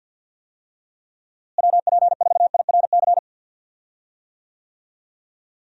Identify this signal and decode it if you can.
Morse code, heard as a keyed tone, about 46 wpm, 700 Hz. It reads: WP4IRC
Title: USA Call Signs - 6 Characters - Repeated 3x - 50wpm